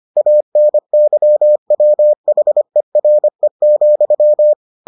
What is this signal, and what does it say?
Morse code, heard as a keyed tone, about 25 wpm, 600 Hz. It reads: ANYWHERE,